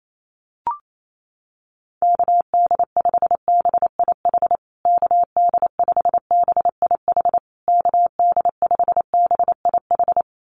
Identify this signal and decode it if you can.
Morse code, heard as a keyed tone, about 28 wpm, 700 Hz. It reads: KD5BIH KD5BIH KD5BIH
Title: USA Call Signs Repeated 3x 28wpm